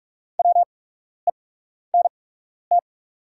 Morse code, heard as a keyed tone, about 43 wpm, spaced as for 12 wpm, 690 Hz. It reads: WENT
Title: 8x Character Spacing - Top 500 Words - 45wpm